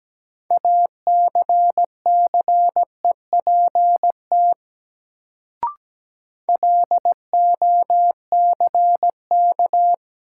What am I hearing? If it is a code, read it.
Morse code, heard as a keyed tone, about 17 wpm, 700 Hz. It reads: ACCEPT LOCK
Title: Sentences from Top 1300 Words - New Word Drill - 17wpm